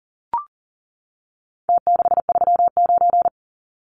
Morse code, heard as a keyed tone, about 40 wpm, 700 Hz. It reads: T639
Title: Numbers - Contest Style - 4-Digit 40wpm